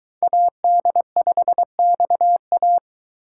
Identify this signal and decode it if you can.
Morse code, heard as a keyed tone, about 22 wpm, 700 Hz. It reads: AD5XA